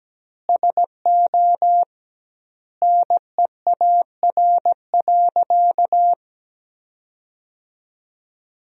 Morse code, heard as a keyed tone, about 17 wpm, 700 Hz. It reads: SO NEAR.
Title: 2x Word-Spacing - Sentences from Top 300 Words - 17wpm